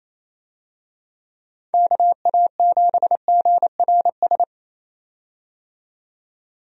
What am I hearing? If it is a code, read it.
Morse code, heard as a keyed tone, about 27 wpm, 700 Hz. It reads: KA7GRS
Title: USA Call Signs - 6 Characters - 28wpm